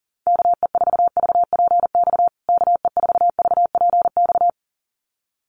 Morse code, heard as a keyed tone, about 37 wpm, 700 Hz. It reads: KE4VPX KE4VPX
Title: USA Call Signs Repeated 2x 40wpm